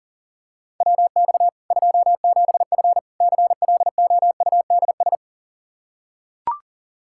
Morse code, heard as a keyed tone, about 40 wpm, 700 Hz. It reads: WX 27F CLOUDS
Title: K1USN SST QSO Practice 40wpm